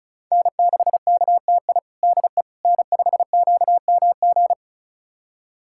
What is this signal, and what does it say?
Morse code, heard as a keyed tone, about 36 wpm, 700 Hz. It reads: N6KTI DE N5QMG
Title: USA Callsign DE Callsign - 5 Characters - 35wpm